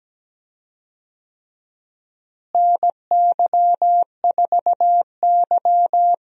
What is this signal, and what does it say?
Morse code, heard as a keyed tone, about 17 wpm, 700 Hz. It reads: NY4Y
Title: USA Call Signs - 4 Characters - Repeated 3x - 17wpm